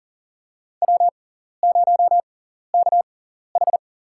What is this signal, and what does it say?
Morse code, heard as a keyed tone, about 40 wpm, spaced as for 14 wpm, 700 Hz. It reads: W0KH
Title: USA Call Signs - 6x Character Spacing - 40wpm